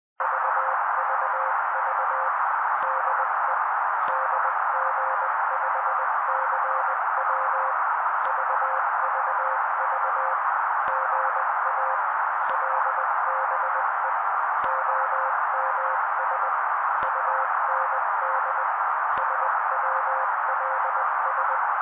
Morse code, heard as a keyed tone, about 20 wpm, 530 Hz. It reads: VVV DE DG5CW VVV GA LBE OMS UND SWLS
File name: daily / 18wpm noise